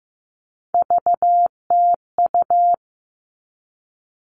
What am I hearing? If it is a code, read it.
Morse code, heard as a keyed tone, about 15 wpm, 700 Hz. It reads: VTU
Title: All Cut Numbers - 3-Digit 15wpm